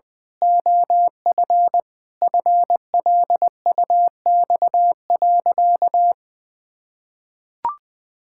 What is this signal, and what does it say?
Morse code, heard as a keyed tone, about 20 wpm, 700 Hz. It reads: OF FLUX.
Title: Pangram Challenge v2 - 20wpm